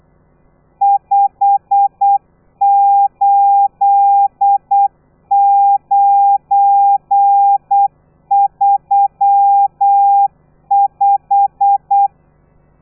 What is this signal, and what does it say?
Morse code, heard as a keyed tone, about 8 wpm, 800 Hz. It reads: 58935